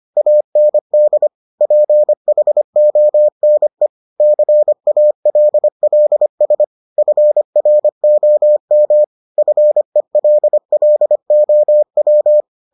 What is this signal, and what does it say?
Morse code, heard as a keyed tone, about 25 wpm, 600 Hz. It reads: AND PHONE CALLS FROM FELLOW